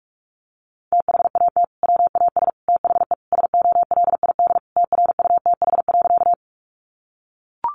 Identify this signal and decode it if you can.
Morse code, heard as a keyed tone, about 44 wpm, 700 Hz. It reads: THAT WAS THE SOLID TRUTH.
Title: Sentences from Top 1500 Words 45wpm